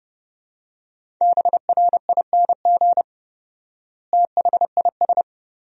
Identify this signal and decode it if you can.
Morse code, heard as a keyed tone, about 30 wpm, 700 Hz. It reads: BRING THIS
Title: 4x Word-Spacing - Sentences from Top 500 Words - 30wpm